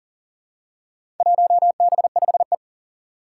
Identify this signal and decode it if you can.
Morse code, heard as a keyed tone, about 39 wpm, 700 Hz. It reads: 1B5E
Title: Four Challenging Letter-Number Combinations 40wpm